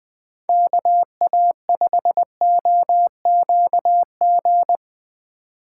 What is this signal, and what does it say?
Morse code, heard as a keyed tone, about 22 wpm, 700 Hz. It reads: KA5OQG
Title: Speed Racing - USA Call Signs 20wpm